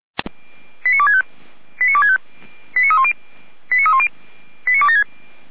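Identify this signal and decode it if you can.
Five-tone selective call (ZVEI), heard as quick runs of five tones, tones of about 80 ms. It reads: 89265 89265 89310 89276
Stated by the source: ZVEI five-tone selective call 89173 received by SDR, from feuerwehr-orlamuende fire brigade audio server.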